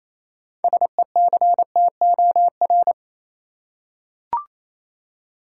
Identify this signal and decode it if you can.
Morse code, heard as a keyed tone, about 28 wpm, 700 Hz. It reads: SECTOR
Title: Six Letter Words - 28wpm